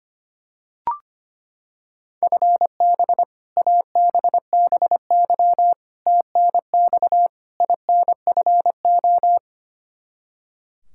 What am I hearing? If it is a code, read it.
Morse code, heard as a keyed tone, about 25 wpm, 700 Hz. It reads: FB ABBY TNX INFO
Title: Practice QSO - Exchange 3 - More Info - 25wpm